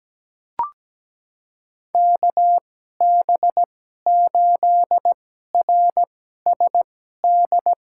Morse code, heard as a keyed tone, about 17 wpm, spaced as for 12 wpm, 700 Hz. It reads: KB8RSD
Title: USA Call Signs - 6 Characters - 2x Character Spacing - 17wpm